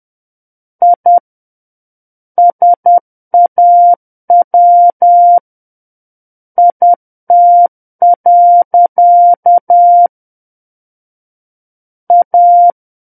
Morse code, heard as a keyed tone, about 10 wpm, 700 Hz. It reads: I SAW IT. A
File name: Book One / 01-Chapter 1 - 10 WPM